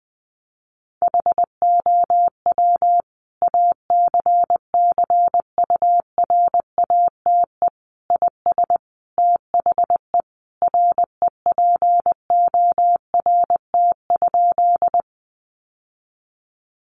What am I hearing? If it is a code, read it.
Morse code, heard as a keyed tone, about 19 wpm, 700 Hz. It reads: HOW ACCURATE IS THE REPORT?
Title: Sentences from Top 1200 Words 20wpm